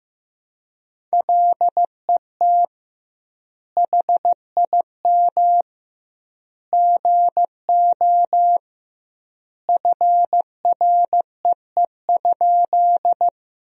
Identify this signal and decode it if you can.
Morse code, heard as a keyed tone, about 15 wpm, 700 Hz. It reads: LET HIM GO FREE?
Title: 2x Word-Spacing - Sentences from Top 500 Words - 15wpm